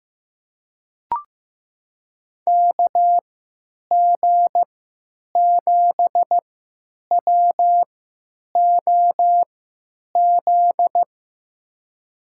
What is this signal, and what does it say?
Morse code, heard as a keyed tone, about 15 wpm, spaced as for 9 wpm, 700 Hz. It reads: KG7WOZ
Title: USA Call Signs - 6 Characters - 3x Character Spacing - 15wpm